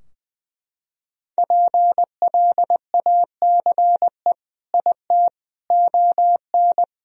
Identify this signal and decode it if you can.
Morse code, heard as a keyed tone, about 21 wpm, 700 Hz. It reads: PLACE IT ON
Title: Sentences from Top 1000 Words - Review Part 2 - 20wpm